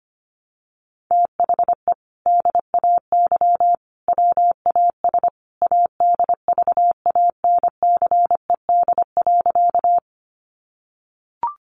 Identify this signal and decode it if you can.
Morse code, heard as a keyed tone, about 25 wpm, 700 Hz. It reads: THE DAY WAS ADVANCED.
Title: Sentences from Top 1200 Words 25wpm